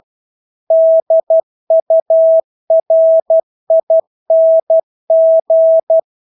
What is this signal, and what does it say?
Morse code, heard as a keyed tone, about 12 wpm, 650 Hz. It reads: DURING
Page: Engelske ord 12 wpm